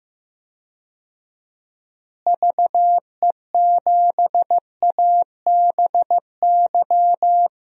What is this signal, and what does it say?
Morse code, heard as a keyed tone, about 15 wpm, 700 Hz. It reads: VE7ABY
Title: DXCC-100 Call Signs - 6 Characters - Repeated 2x - 15wpm